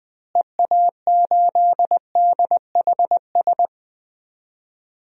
Morse code, heard as a keyed tone, about 20 wpm, 700 Hz. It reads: EA8DHS
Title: DXCC-100 Call Signs - 6 Characters - Repeated 2x - 20wpm